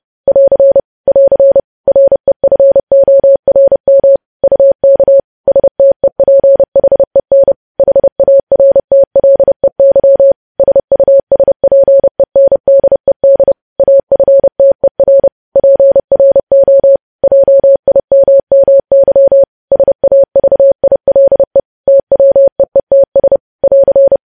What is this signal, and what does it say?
Morse code, heard as a keyed tone, about 30 wpm, 550 Hz. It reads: <AR> <AR> REFORM UK STEPHEN HARTLEY SUSPENDED AFTER PRO JIMMY SAVILE TWEETS <AR>
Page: news headlines in morse code at 30 wpm